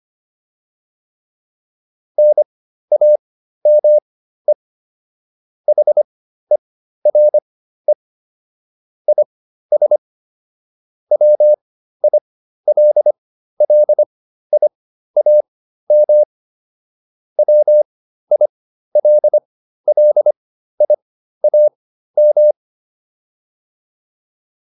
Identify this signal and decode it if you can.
Morse code, heard as a keyed tone, about 25 wpm, spaced as for 13 wpm, 600 Hz. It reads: NAME HERE IS WILLIAM WILLIAM